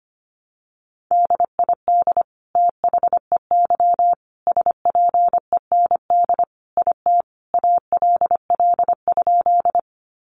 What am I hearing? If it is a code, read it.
Morse code, heard as a keyed tone, about 25 wpm, 700 Hz. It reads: DID THEY SPEND IT ALL?
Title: Sentences from Top 800 Words 25wpm